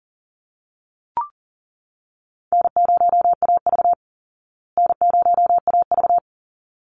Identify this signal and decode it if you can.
Morse code, heard as a keyed tone, about 38 wpm, 700 Hz. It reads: N0AV N0AV
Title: Easy Call Signs Repeated 2x - 40wpm